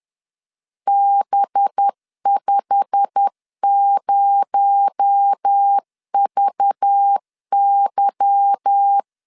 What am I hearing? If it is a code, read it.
Morse code, heard as a keyed tone, about 10 wpm, 800 Hz. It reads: B50VY
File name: LEZIONI / 10 SIGNS / 10B INT 10wpm 2844